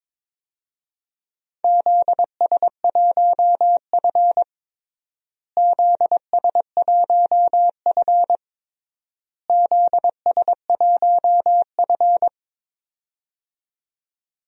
Morse code, heard as a keyed tone, about 22 wpm, 700 Hz. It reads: ZS1F ZS1F ZS1F
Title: DXCC-100 Call Signs - 4 Characters - Repeated 3x - 22wpm